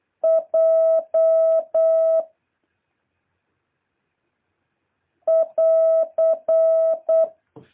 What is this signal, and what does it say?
Morse code, heard as a keyed tone, about 8 wpm, 650 Hz. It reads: J <AR>
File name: jh8cbh / 2023cw